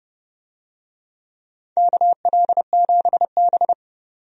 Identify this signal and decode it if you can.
Morse code, heard as a keyed tone, about 31 wpm, 700 Hz. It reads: KL7B
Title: DXCC-100 Call Signs - 4 Characters - 30wpm